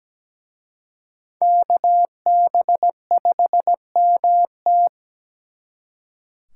Morse code, heard as a keyed tone, about 20 wpm, 700 Hz. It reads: KB5MT
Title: Speed Racing - USA 2x2 Call Signs 17wpm